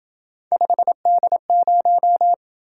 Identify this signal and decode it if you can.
Morse code, heard as a keyed tone, about 27 wpm, 700 Hz. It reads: 5D0